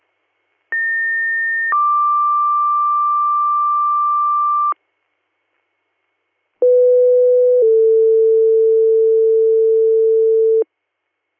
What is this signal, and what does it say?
Two-tone page, heard as a one-second tone, then a three-second tone, about 1800 then 1180 Hz, about 500 then 450 Hz. It is heard from 0.7 to 10.6 s.